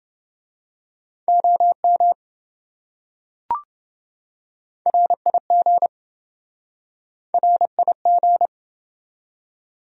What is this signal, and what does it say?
Morse code, heard as a keyed tone, about 27 wpm, 700 Hz. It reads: OM RIG RIG